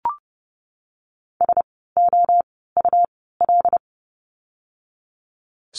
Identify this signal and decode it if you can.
Morse code, heard as a keyed tone, about 30 wpm, spaced as for 17 wpm, 700 Hz. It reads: SOUL
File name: Lesson 17f - Letter u - Words 30wpm